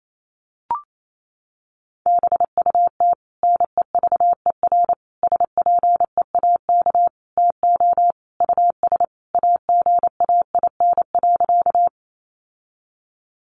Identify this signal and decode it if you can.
Morse code, heard as a keyed tone, about 28 wpm, 700 Hz. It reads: BUT NEVER SPEAK TO US AGAIN.
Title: Sentences from Top 1100 Words 28wpm